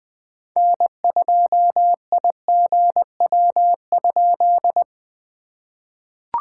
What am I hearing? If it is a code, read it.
Morse code, heard as a keyed tone, about 20 wpm, 700 Hz. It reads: N2IGW?
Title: POTA QSO Practice - 20wpm